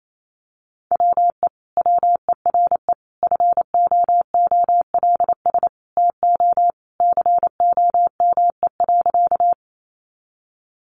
Morse code, heard as a keyed tone, about 28 wpm, 700 Hz. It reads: WE WERE FOOLS TO COME.
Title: A Sound Of Thunder 28wpm